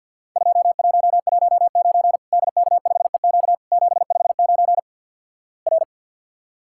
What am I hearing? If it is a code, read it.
Morse code, heard as a keyed tone, about 50 wpm, 700 Hz. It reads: 2119 DK5EQ 759 R
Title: Worked All Europe DX Contest - QTC Exchange - 50wpm